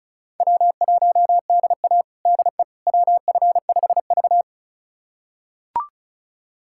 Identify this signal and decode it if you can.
Morse code, heard as a keyed tone, about 34 wpm, 700 Hz. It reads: W1DA DE WF5V
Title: USA Callsign DE Callsign - 4 Characters - 35wpm